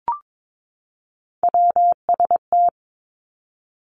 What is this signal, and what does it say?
Morse code, heard as a keyed tone, about 22 wpm, 700 Hz. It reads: WST
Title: Three Challenging Letter-Number Combinations 22wpm